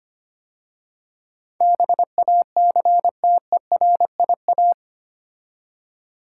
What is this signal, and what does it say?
Morse code, heard as a keyed tone, about 25 wpm, 700 Hz. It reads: BACTERIA
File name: 8 Letter Words - First 3 Letters Given - 25wpm